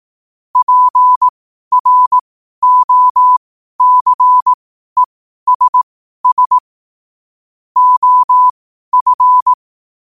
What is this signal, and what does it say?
Morse code, heard as a keyed tone, about 18 wpm, spaced as for 12 wpm, 1000 Hz. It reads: PROCESS OF
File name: qotdfiles / 12wpm